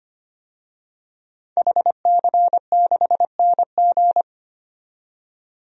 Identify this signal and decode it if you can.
Morse code, heard as a keyed tone, about 25 wpm, 700 Hz. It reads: HC6NG
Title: DXCC-100 Call Signs - 5 Characters - 25wpm